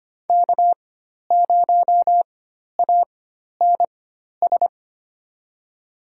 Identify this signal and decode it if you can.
Morse code, heard as a keyed tone, about 25 wpm, spaced as for 12 wpm, 700 Hz. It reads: K0ANS
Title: USA Call Signs - 5 Characters - 4x Character Spacing - 25wpm